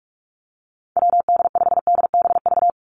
Morse code, heard as a keyed tone, about 44 wpm, 700 Hz. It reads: WD5DBV
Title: Call Signs - 45wpm